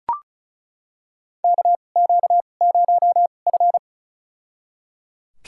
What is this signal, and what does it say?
Morse code, heard as a keyed tone, about 34 wpm, spaced as for 25 wpm, 700 Hz. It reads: KQ0F
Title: USA Call Signs - 4 Characters - 2x Character Spacing - 35wpm